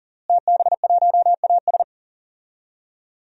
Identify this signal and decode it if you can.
Morse code, heard as a keyed tone, about 39 wpm, 700 Hz. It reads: TB1AS
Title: DXCC-100 Call Signs - 5 Characters - Repeated 2x - 40wpm